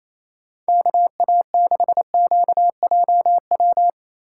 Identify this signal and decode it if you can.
Morse code, heard as a keyed tone, about 28 wpm, 700 Hz. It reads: KA6QJW
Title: USA Call Signs - 2x Character Spacing - 28wpm